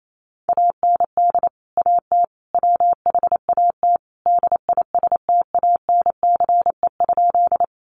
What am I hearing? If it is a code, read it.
Morse code, heard as a keyed tone, about 28 wpm, 700 Hz. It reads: AND AT WHAT DISTANCE?
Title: Sentences from Top 1200 Words 28wpm